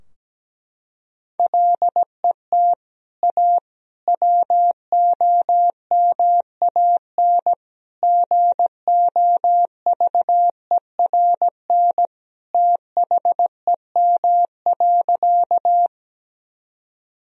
Morse code, heard as a keyed tone, about 17 wpm, 700 Hz. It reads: LET A WOMAN GOVERN THEM.
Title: Sentences from Top 1000 Words - Review Part 2 - 17wpm